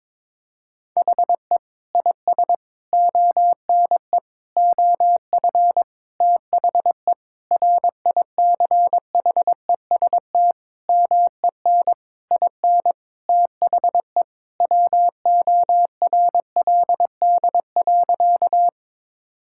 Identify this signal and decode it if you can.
Morse code, heard as a keyed tone, about 22 wpm, 700 Hz. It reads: HE IS ONE OF THE RICHEST MEN IN THE WORLD.
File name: practice / 22wpm / Sentences with 8-Word Phrases - 22wpm